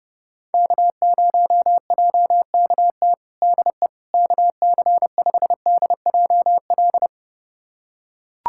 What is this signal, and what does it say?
Morse code, heard as a keyed tone, about 30 wpm, 700 Hz. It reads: K0JKT DE KC5DJL
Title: USA Callsign DE Callsign 30wpm